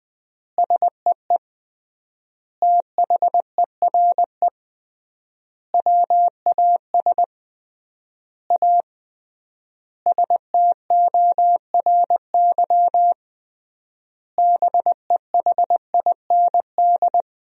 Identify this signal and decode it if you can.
Morse code, heard as a keyed tone, about 20 wpm, 700 Hz. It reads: SEE THERE WAS A STORY BEHIND
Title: 3x Word-Spacing - Sentences from Top 500 Words - 20wpm